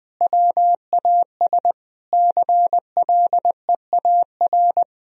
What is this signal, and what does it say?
Morse code, heard as a keyed tone, about 20 wpm, 700 Hz. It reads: WAS CLEAR
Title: Sentences from Top 1000 Words - Review Part 2 - 20wpm